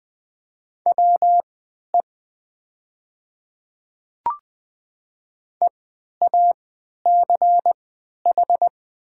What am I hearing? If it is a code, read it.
Morse code, heard as a keyed tone, about 20 wpm, spaced as for 11 wpm, 700 Hz. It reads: WE EACH
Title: 3x Character Spacing - Top 100 Words - 20wpm